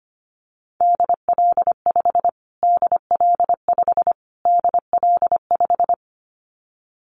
Morse code, heard as a keyed tone, about 25 wpm, 700 Hz. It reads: DL5 DL5 DL5
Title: Callsign Prefixes for Top 25 DXCC - Repeated 3x - 25wpm